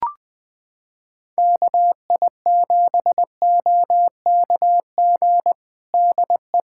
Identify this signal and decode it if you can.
Morse code, heard as a keyed tone, about 21 wpm, 700 Hz. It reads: KI7OKG DE
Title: Practice QSO - Exchange 4 - Ending - 20wpm